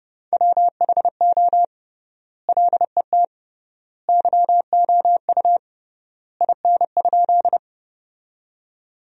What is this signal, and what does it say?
Morse code, heard as a keyed tone, about 32 wpm, 700 Hz. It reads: WHO LET YOU IN?
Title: 3x Word-Spacing - Sentences from Top 300 Words - 30wpm